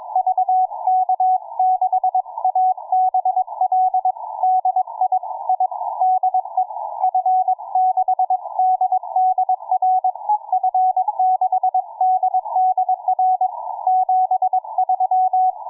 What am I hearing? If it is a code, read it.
Morse code, heard as a keyed tone, about 22 wpm, 740 Hz. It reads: VK6ABL DI I DE F6DDR F6DDR 73